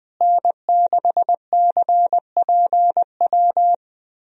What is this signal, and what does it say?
Morse code, heard as a keyed tone, about 20 wpm, 700 Hz. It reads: N6CPW
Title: USA Call Signs - 5 Characters - Repeated 2x - 20wpm